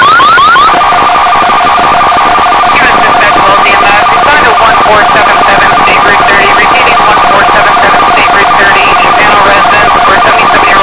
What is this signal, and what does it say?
Two-tone teleprinter signal, about 450 Hz shift, with speech.